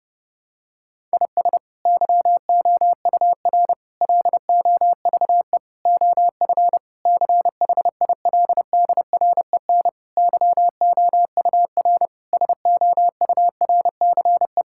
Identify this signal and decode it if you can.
Morse code, heard as a keyed tone, about 28 wpm, 700 Hz. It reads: IS YOUR LOVE OF CHILDREN YOUR SOURCE
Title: Sentences from Top 1000 Words - Review Part 5 - 30wpm